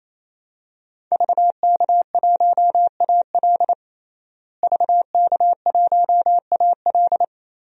Morse code, heard as a keyed tone, about 28 wpm, 700 Hz. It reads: VK1AL VK1AL
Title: DXCC-100 Call Signs - 5 Characters - Repeated 3x - 28wpm